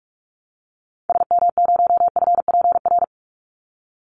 Morse code, heard as a keyed tone, about 44 wpm, 700 Hz. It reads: SM0FPR
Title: Call Signs Repeated 2x - 45wpm